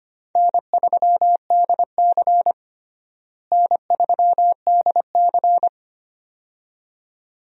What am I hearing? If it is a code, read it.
Morse code, heard as a keyed tone, about 25 wpm, 700 Hz. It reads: N3DC N3DC
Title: USA Call Signs - 4 Characters - Repeated 2x - 25wpm